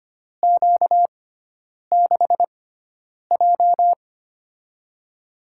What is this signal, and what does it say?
Morse code, heard as a keyed tone, about 25 wpm, spaced as for 9 wpm, 700 Hz. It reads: Q6J
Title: USA Call Signs - 4 Characters - 6x Character Spacing - 25wpm